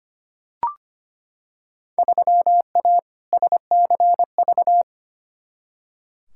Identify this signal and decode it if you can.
Morse code, heard as a keyed tone, about 25 wpm, 700 Hz. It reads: 3A SCV
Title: Field Day Exchanges - 25wpm